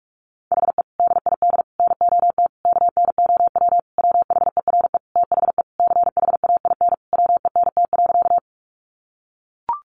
Morse code, heard as a keyed tone, about 44 wpm, 700 Hz. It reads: HE DID NOT KNOW WHERE THE CHAIN WENT.
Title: Sentences from Top 1500 Words 45wpm